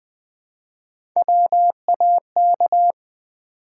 Morse code, heard as a keyed tone, about 20 wpm, 700 Hz. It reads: WAK
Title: Build A Word - 1 Letter At A Time - 4-Letter Words - 20wpm